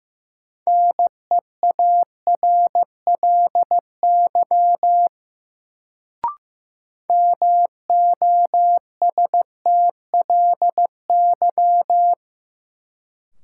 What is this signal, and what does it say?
Morse code, heard as a keyed tone, about 15 wpm, 700 Hz. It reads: NEARLY MOSTLY
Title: Sentences from Top 1300 Words - New Word Drill Alt - 15wpm